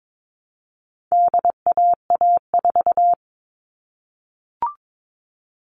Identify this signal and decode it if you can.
Morse code, heard as a keyed tone, about 22 wpm, 700 Hz. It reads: DAA4
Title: All Cut Numbers - 4-Digit 22wpm